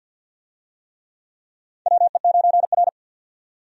Morse code, heard as a keyed tone, about 52 wpm, 700 Hz. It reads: WE9R
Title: USA Call Signs - 4 Characters - 50wpm